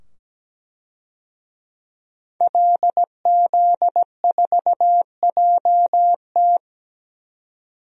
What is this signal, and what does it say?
Morse code, heard as a keyed tone, about 17 wpm, 700 Hz. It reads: LZ4JT
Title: DXCC-100 Call Signs - 5 Characters - 17wpm